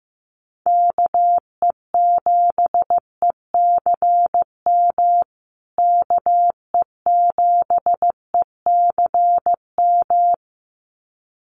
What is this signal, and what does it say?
Morse code, heard as a keyed tone, about 15 wpm, 700 Hz. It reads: KE7ECM KE7ECM
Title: USA Call Signs Repeated 2x 15wpm